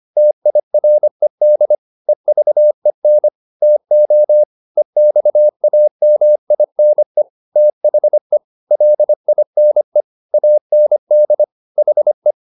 Morse code, heard as a keyed tone, about 25 wpm, 600 Hz. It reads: TIRED EVEN TO EXAMINE THE LINE AND HE
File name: the old man and the sea part 1 25wpm